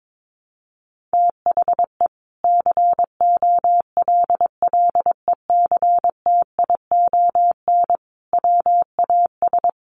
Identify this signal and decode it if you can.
Morse code, heard as a keyed tone, about 22 wpm, 700 Hz. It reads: THE COLLECTION WAS
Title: Sentences from Top 1200 Words 22wpm